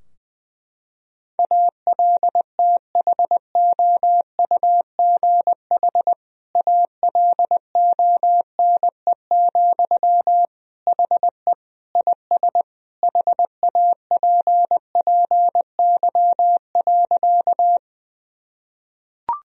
Morse code, heard as a keyed tone, about 21 wpm, 700 Hz. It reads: ALTHOUGH ALONE, HE IS HAPPY.
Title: Sentences from Top 1000 Words - Review Part 3 - 20wpm